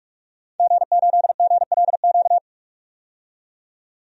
Morse code, heard as a keyed tone, about 46 wpm, 700 Hz. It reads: G8GLQ
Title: DXCC-100 Call Signs - 5 Characters - 45wpm